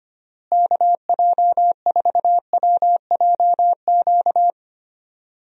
Morse code, heard as a keyed tone, about 25 wpm, 700 Hz. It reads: KJ4WJQ
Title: USA Call Signs - 6 Characters - Repeated 2x - 25wpm